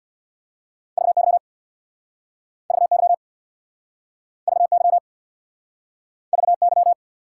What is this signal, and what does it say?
Morse code, heard as a keyed tone, about 69 wpm, 700 Hz. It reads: VY VY VY VY